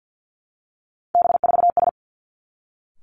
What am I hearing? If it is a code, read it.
Morse code, heard as a keyed tone, about 50 wpm, 700 Hz. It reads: B4S
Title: Three Challenging Letter-Number Combinations 50wpm